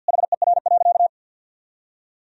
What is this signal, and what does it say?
Morse code, heard as a keyed tone, about 51 wpm, 700 Hz. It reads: HER.
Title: Sentences from Top 1000 Words - Review Part 4 - 50wpm